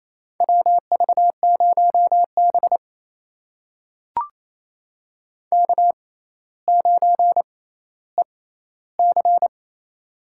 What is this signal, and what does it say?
Morse code, heard as a keyed tone, about 28 wpm, spaced as for 14 wpm, 700 Hz. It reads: WV0B K9EC
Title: USA Call Signs - 4 Characters - 6x Character Spacing - 28wpm